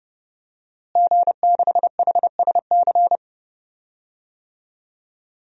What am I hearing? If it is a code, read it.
Morse code, heard as a keyed tone, about 32 wpm, 700 Hz. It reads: G6HSC
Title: DXCC-100 Call Signs - 5 Characters - Repeated 2x - 30wpm